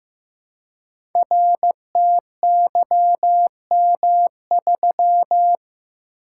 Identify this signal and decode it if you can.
Morse code, heard as a keyed tone, about 15 wpm, 700 Hz. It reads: RTYM3
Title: Code Groups - 5 Letter:Numbers - 15wpm